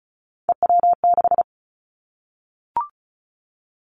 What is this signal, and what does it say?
Morse code, heard as a keyed tone, about 34 wpm, 700 Hz. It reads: EW6
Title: Callsign Prefixes for Top 100 DXCC - 35wpm